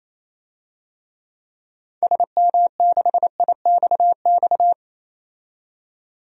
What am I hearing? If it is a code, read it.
Morse code, heard as a keyed tone, about 28 wpm, 700 Hz. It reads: SM6IXX
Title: DXCC-100 Call Signs - 6 Characters - 28wpm